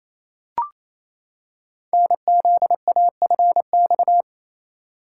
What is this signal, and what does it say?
Morse code, heard as a keyed tone, about 27 wpm, 700 Hz. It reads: NZAFX
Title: Code Groups - 5 Letters - 28wpm